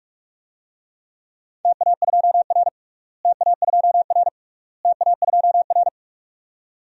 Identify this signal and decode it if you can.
Morse code, heard as a keyed tone, about 46 wpm, 700 Hz. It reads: TA2R TA2R TA2R
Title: DXCC-100 Call Signs - 4 Characters - Repeated 3x - 45wpm